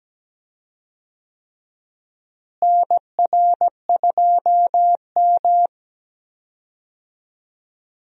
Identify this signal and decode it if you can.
Morse code, heard as a keyed tone, about 17 wpm, 700 Hz. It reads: NR2M
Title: USA Call Signs - 4 Characters - Repeated 3x - 17wpm